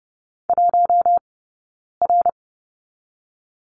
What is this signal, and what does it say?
Morse code, heard as a keyed tone, about 30 wpm, 700 Hz. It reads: 1 R
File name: Lesson 18d - Letter m - Two Characters 30wpm